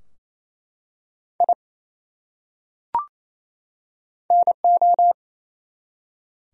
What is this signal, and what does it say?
Morse code, heard as a keyed tone, about 28 wpm, 700 Hz. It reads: I NO
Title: Top 100 Words 28wpm